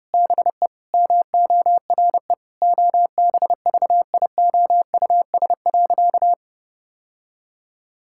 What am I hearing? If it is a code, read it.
Morse code, heard as a keyed tone, about 30 wpm, 700 Hz. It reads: BE MORE OBVIOUS.
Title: Sentences from Top 1600 Words 30wpm